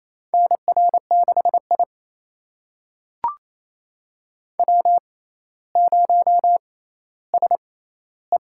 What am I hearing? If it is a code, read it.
Morse code, heard as a keyed tone, about 28 wpm, spaced as for 14 wpm, 700 Hz. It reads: NR6I W0SE
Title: USA Call Signs - 4 Characters - 6x Character Spacing - 28wpm